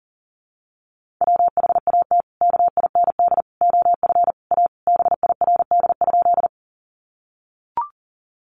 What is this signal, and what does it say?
Morse code, heard as a keyed tone, about 40 wpm, 700 Hz. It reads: WHAT KIND OF A BIRD?
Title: Sentences from Top 400 Words 40wpm